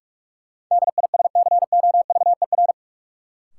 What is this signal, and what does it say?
Morse code, heard as a keyed tone, about 42 wpm, 700 Hz. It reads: DISCOVER
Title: Sentences from Top 1100 Words - New Word Drill Alt - 45wpm